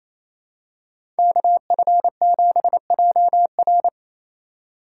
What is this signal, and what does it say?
Morse code, heard as a keyed tone, about 28 wpm, 700 Hz. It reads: KF7JR
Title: USA Call Signs - 4x Character Spacing - 28wpm